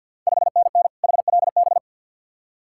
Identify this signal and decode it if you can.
Morse code, heard as a keyed tone, about 51 wpm, 690 Hz. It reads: 5NN HLB
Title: Ontario QSO Party Exchange 50wpm